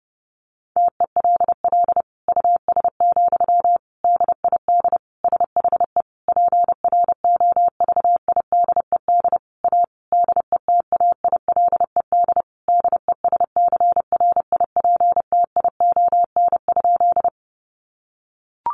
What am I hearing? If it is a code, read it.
Morse code, heard as a keyed tone, about 30 wpm, 700 Hz. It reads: TELL US, DID SHE PROVIDED A DETAILED DESCRIPTION?
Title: Sentences from Top 1500 Words 30wpm